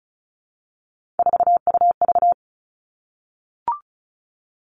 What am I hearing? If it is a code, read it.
Morse code, heard as a keyed tone, about 34 wpm, 700 Hz. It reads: 4UV
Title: Three Challenging Letter-Number Combinations 35wpm